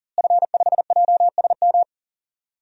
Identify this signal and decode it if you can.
Morse code, heard as a keyed tone, about 39 wpm, 700 Hz. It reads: F5JSM
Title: DXCC-100 Call Signs - 5 Characters - Repeated 3x - 40wpm